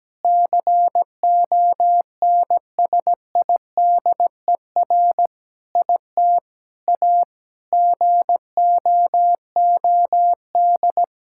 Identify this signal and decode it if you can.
Morse code, heard as a keyed tone, about 17 wpm, 700 Hz. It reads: CONSIDER IT A GOOD
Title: Sentences from Top 1600 Words 17wpm